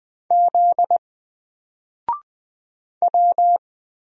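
Morse code, heard as a keyed tone, about 20 wpm, 700 Hz. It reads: Z W
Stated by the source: Single letter and number Morse drill at 20 wpm, from N4LCD.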